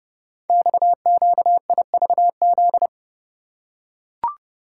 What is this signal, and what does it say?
Morse code, heard as a keyed tone, about 28 wpm, 700 Hz. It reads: XQIVZ
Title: Code Groups - 5 Letters - 30wpm